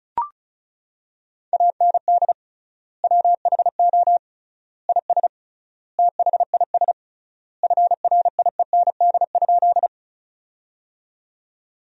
Morse code, heard as a keyed tone, about 36 wpm, 700 Hz. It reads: AND WHO IS THIS FRIEND?
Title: 3x Word-Spacing - Sentences from Top 300 Words - 35wpm